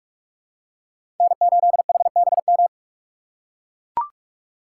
Morse code, heard as a keyed tone, about 40 wpm, 700 Hz. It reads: N8HBM